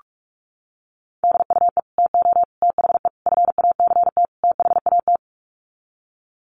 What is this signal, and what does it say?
Morse code, heard as a keyed tone, about 44 wpm, 700 Hz. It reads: DUE TO THE FACT THAT
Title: Sets of 5 Words Encore 45wpm